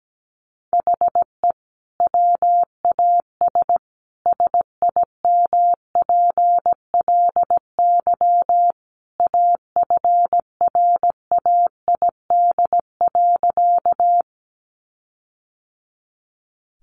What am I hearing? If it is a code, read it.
Morse code, heard as a keyed tone, about 17 wpm, 700 Hz. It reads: HE WAS SIMPLY AFRAID.
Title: Sentences from Top 1500 Words 17wpm